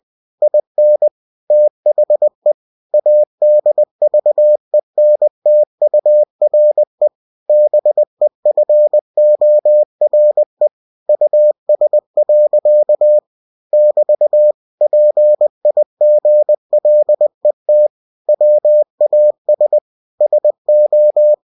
Morse code, heard as a keyed tone, about 20 wpm, 600 Hz. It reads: IN THE ADVENTURE BEFORE US. <BT> PIGLET WAS SO
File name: cw-books / winnie-the-pooh / winnie-20wpm-page-0062